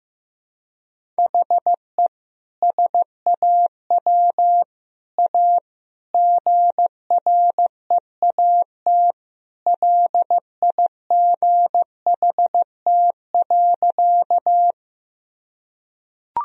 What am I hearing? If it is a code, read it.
Morse code, heard as a keyed tone, about 15 wpm, 700 Hz. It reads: HE SAW A GREAT LIGHT.
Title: Sentences from Top 1000 Words - Review Part 1 - 15wpm